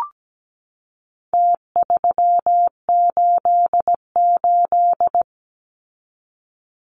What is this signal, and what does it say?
Morse code, heard as a keyed tone, about 17 wpm, 700 Hz. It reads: T388
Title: Numbers - Contest Style - 4-Digit 17wpm